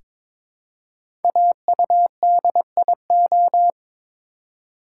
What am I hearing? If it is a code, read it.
Morse code, heard as a keyed tone, about 22 wpm, 700 Hz. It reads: AUDIO
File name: 5 Letter Words - First 3 Letters Given - 22wpm